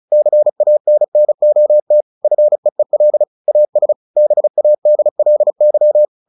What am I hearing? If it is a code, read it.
Morse code, heard as a keyed tone, about 34 wpm, 600 Hz. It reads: CANNOT FEEL AS BADLY
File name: the old man and the sea / the old man and the sea 1 35wpm